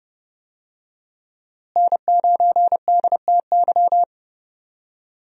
Morse code, heard as a keyed tone, about 30 wpm, 700 Hz. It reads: N9DTY
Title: USA Call Signs - 5 Characters - Repeated 3x - 30wpm